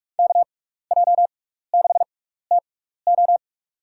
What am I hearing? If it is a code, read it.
Morse code, heard as a keyed tone, about 44 wpm, spaced as for 15 wpm, 700 Hz. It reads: KJ6TO
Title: USA Call Signs - 5 Characters - 6x Character Spacing - 45wpm